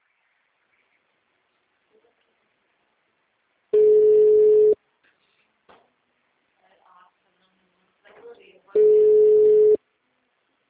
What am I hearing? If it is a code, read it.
Ringback tone (Europe, 425 Hz), with speech.
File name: out- -2032- - - .4709